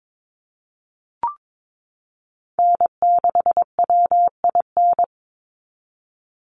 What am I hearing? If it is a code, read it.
Morse code, heard as a keyed tone, about 22 wpm, 700 Hz. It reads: N6WIN
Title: Call Signs Repeated 2x - 22wpm